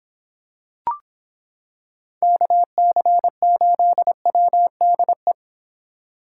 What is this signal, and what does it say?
Morse code, heard as a keyed tone, about 26 wpm, 700 Hz. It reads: KC8WDE